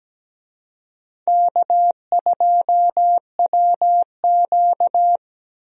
Morse code, heard as a keyed tone, about 17 wpm, 700 Hz. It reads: K2WQ
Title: USA Call Signs - 4 Characters - 17wpm